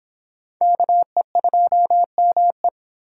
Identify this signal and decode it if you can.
Morse code, heard as a keyed tone, about 24 wpm, 700 Hz. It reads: KE2ME